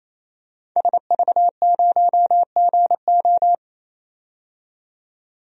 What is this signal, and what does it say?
Morse code, heard as a keyed tone, about 28 wpm, 700 Hz. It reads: SV0GO
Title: DXCC-100 Call Signs - 5 Characters - 28wpm